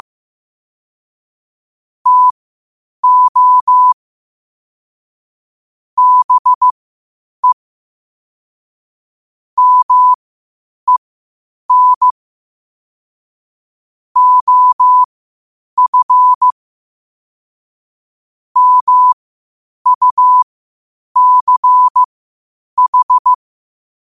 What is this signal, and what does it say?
Morse code, heard as a keyed tone, about 15 wpm, spaced as for 8 wpm, 1000 Hz. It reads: TO BE MEN OF MUCH
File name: qotdfiles / 8wpm